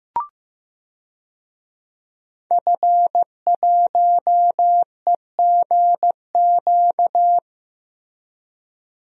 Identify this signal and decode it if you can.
Morse code, heard as a keyed tone, about 15 wpm, 700 Hz. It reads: F1EGQ
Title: DXCC-100 Call Signs - 5 Characters - Repeated 2x - 15wpm